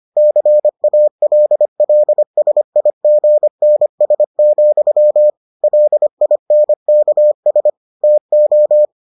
Morse code, heard as a keyed tone, about 25 wpm, 600 Hz. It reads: CALLSIGNS, LINKS TO